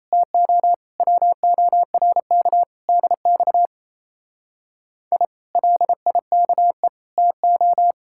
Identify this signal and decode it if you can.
Morse code, heard as a keyed tone, about 33 wpm, 700 Hz. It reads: TO WORK DX I LIKE TO
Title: Speed-Racing Common QSO Phrases - 28wpm